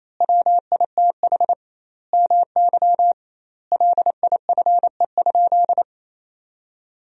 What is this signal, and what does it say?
Morse code, heard as a keyed tone, about 28 wpm, 700 Hz. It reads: WITH MY LIFE?
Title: 2x Word-Spacing - Sentences from Top 300 Words - 28wpm